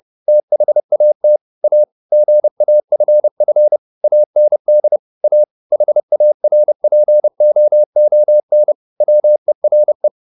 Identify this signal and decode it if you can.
Morse code, heard as a keyed tone, about 30 wpm, 600 Hz. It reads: THAT A GAFF AND A HARPOON WERE
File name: the old man and the sea / the old man and the sea 1 30wpm